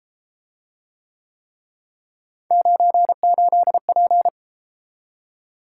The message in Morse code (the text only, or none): 98P